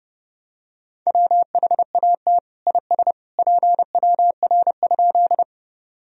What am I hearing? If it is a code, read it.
Morse code, heard as a keyed tone, about 32 wpm, 700 Hz. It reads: WHAT IS PWR?
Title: Sentences from Top 500 Words and Abbreviations 30wpm